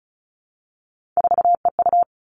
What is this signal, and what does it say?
Morse code, heard as a keyed tone, about 36 wpm, 700 Hz. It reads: 4EU
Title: Three Challenging Letter-Number Combinations 35wpm